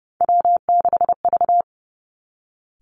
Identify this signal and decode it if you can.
Morse code, heard as a keyed tone, about 30 wpm, 700 Hz. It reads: W6V
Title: Three Challenging Letter-Number Combinations 30wpm